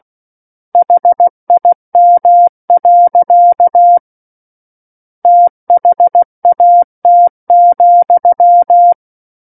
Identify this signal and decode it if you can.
Morse code, heard as a keyed tone, about 16 wpm, 700 Hz. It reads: HIM. THAT,
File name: Book One / 07-Chapter 7 - 16 WPM